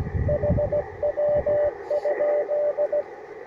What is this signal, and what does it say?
Morse code, heard as a keyed tone, about 17 wpm, 590 Hz, with radio noise. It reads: HW?
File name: attachments / UR4MCK IZ5GVP